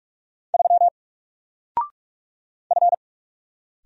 Morse code, heard as a keyed tone, about 46 wpm, 700 Hz. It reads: 3 F
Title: Single Letter-Number 45wpm